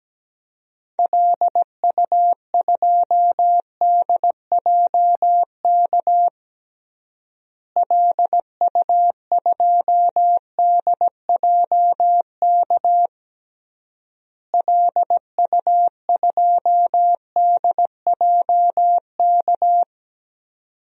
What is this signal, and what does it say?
Morse code, heard as a keyed tone, about 17 wpm, 700 Hz. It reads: LU2DJK LU2DJK LU2DJK
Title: DXCC-100 Call Signs - 6 Characters - Repeated 3x - 17wpm